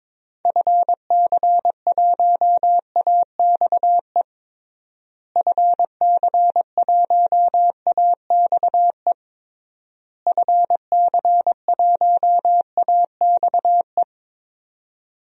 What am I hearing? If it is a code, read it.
Morse code, heard as a keyed tone, about 21 wpm, 700 Hz. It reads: FC1AXE FC1AXE FC1AXE
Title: DXCC-100 Call Signs - 6 Characters - Repeated 3x - 22wpm